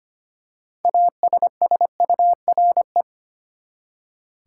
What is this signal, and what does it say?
Morse code, heard as a keyed tone, about 25 wpm, 700 Hz. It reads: ASSURE
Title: Six Letter Words - 25wpm